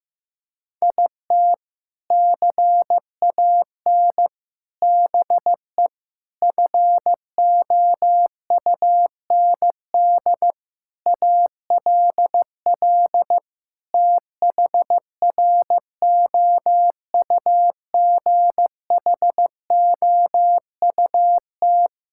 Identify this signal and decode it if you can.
Morse code, heard as a keyed tone, about 15 wpm, 700 Hz. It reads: IT CAN BE FOUND ALL THROUGHOUT
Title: Sentences from Top 1000 Words - Review Part 5 - 15wpm